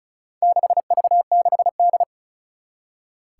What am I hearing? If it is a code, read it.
Morse code, heard as a keyed tone, about 34 wpm, 700 Hz. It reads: 6V6D
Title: Four Challenging Letter-Number Combinations 35wpm